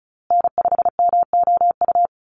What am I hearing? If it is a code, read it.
Morse code, heard as a keyed tone, about 34 wpm, 700 Hz. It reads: N5MOU
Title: USA Call Signs Repeated 3x 35wpm